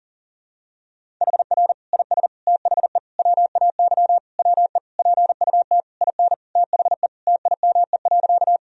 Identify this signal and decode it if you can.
Morse code, heard as a keyed tone, about 37 wpm, 700 Hz. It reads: HR IS THE WAY WE PUT IN THE TIME.
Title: Sentences from Top 200 Words and Abbreviations 40wpm